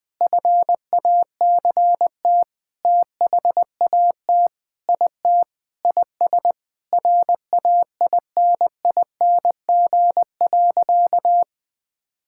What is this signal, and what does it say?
Morse code, heard as a keyed tone, about 21 wpm, 700 Hz. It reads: FACT THAT IT IS RAINING.
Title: Sentences with 8-Word Phrases - Spoken Beginning - 20wpm